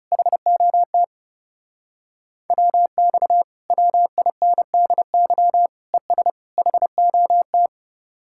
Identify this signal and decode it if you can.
Morse code, heard as a keyed tone, about 38 wpm, 700 Hz. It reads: HOT WX WINDY ES HOT